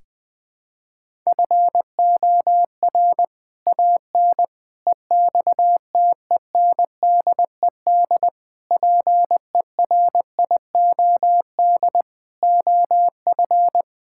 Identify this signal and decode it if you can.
Morse code, heard as a keyed tone, about 21 wpm, 700 Hz. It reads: FOR AN EXTENDED PERIOD OF
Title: Sets of 6 Words - 20wpm